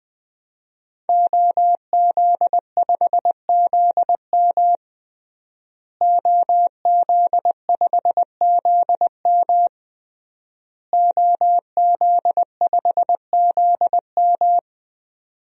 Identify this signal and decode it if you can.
Morse code, heard as a keyed tone, about 20 wpm, 700 Hz. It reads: OZ5ZM OZ5ZM OZ5ZM
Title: DXCC-100 Call Signs - 5 Characters - Repeated 3x - 20wpm